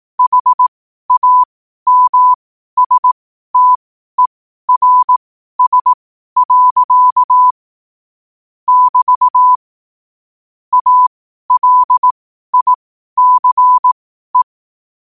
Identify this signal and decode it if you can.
Morse code, heard as a keyed tone, about 18 wpm, spaced as for 12 wpm, 1000 Hz. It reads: HAMSTERS. <BT> ALICE
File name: qotdfiles / 12wpm